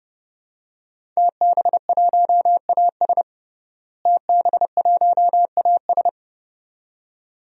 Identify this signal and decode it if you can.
Morse code, heard as a keyed tone, about 30 wpm, 700 Hz. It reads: TB1AS TB1AS
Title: DXCC-100 Call Signs - 5 Characters - Repeated 2x - 30wpm